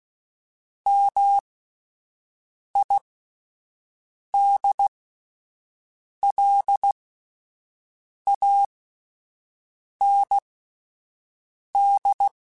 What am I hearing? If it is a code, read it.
Morse code, heard as a keyed tone, about 16 wpm, spaced as for 5 wpm, 800 Hz. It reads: MIDLAND